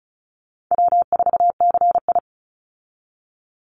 Morse code, heard as a keyed tone, about 34 wpm, 700 Hz. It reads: W4CI
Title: Easy Call Signs Repeated 2x - 35wpm